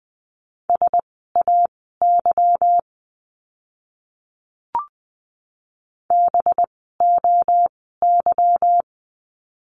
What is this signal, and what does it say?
Morse code, heard as a keyed tone, about 20 wpm, spaced as for 15 wpm, 700 Hz. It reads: SAY BOY
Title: 185 Lesson 30f - Letter b - Words 20wpm